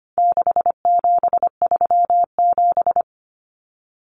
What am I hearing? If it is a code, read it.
Morse code, heard as a keyed tone, about 25 wpm, 700 Hz. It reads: 6737
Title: Common Cut Numbers - 4-Digit 25wpm